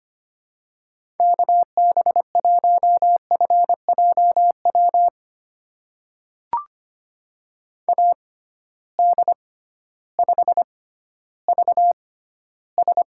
Morse code, heard as a keyed tone, about 25 wpm, spaced as for 14 wpm, 700 Hz. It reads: KB1FJW AD5VS
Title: USA Call Signs - 6x Character Spacing - 25wpm